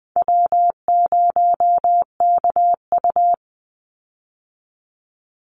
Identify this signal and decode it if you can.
Morse code, heard as a keyed tone, about 20 wpm, 700 Hz. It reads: W0KU
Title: Easy Call Signs Repeated 2x - 20wpm